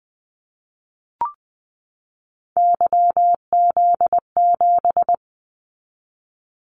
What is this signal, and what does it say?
Morse code, heard as a keyed tone, about 20 wpm, 700 Hz. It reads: YZ7
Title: Callsign Prefixes for Top 25 DXCC - 20wpm